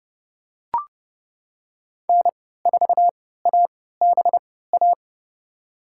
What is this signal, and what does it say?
Morse code, heard as a keyed tone, about 32 wpm, spaced as for 17 wpm, 700 Hz. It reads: N4ABA
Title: USA Call Signs - 5 Characters - 3x Character Spacing - 30wpm